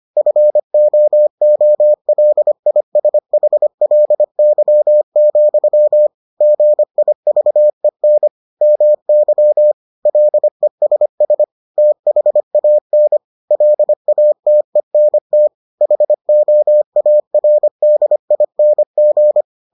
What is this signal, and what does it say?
Morse code, heard as a keyed tone, about 25 wpm, 600 Hz. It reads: FOOLISHLY, GIVEN MY LESS THAN LATENT HOARDING